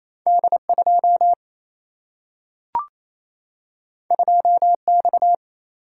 Morse code, heard as a keyed tone, about 28 wpm, 700 Hz. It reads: D2 2X
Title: Single-Letter Single-Number 28wpm